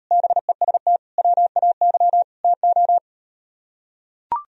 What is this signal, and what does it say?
Morse code, heard as a keyed tone, about 39 wpm, 700 Hz. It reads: BEST WAY TO